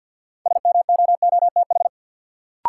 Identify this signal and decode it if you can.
Morse code, heard as a keyed tone, about 46 wpm, 700 Hz. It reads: SMOOTH
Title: Sentences from Top 1400 Words - New Word Drill - 50wpm